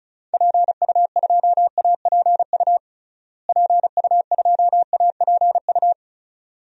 Morse code, heard as a keyed tone, about 36 wpm, 700 Hz. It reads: PU2APU PU2APU
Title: DXCC-100 Call Signs - 6 Characters - Repeated 2x - 35wpm